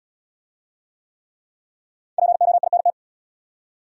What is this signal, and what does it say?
Morse code, heard as a keyed tone, about 71 wpm, 700 Hz. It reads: POINT